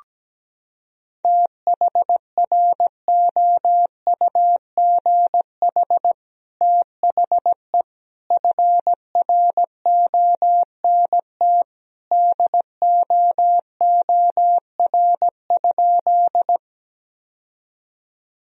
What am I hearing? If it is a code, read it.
Morse code, heard as a keyed tone, about 17 wpm, 700 Hz. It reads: THROUGH THE FRONT DOOR?
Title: Sentences from Top 1000 Words - Review Part 2 - 17wpm